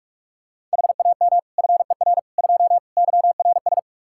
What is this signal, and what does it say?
Morse code, heard as a keyed tone, about 47 wpm, 700 Hz. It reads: HAM FER 2 YRS
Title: Practice QSO - Exchange 3 - More Info - 45wpm